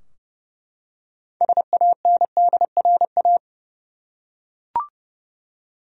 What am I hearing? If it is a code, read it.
Morse code, heard as a keyed tone, about 30 wpm, 700 Hz. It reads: SANDRA
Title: USA Common Names - 30wpm